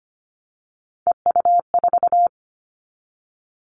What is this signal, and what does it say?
Morse code, heard as a keyed tone, about 25 wpm, 700 Hz. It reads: EU4
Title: Callsign Prefixes for Top 100 DXCC - 25wpm